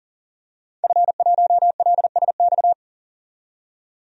Morse code, heard as a keyed tone, about 40 wpm, 700 Hz. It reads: F1LSX
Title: DXCC-100 Call Signs - 5 Characters - 40wpm